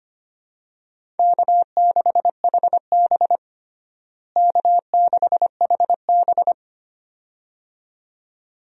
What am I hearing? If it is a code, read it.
Morse code, heard as a keyed tone, about 25 wpm, 700 Hz. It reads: K6HB K6HB
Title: USA Call Signs - 4 Characters - Repeated 2x - 25wpm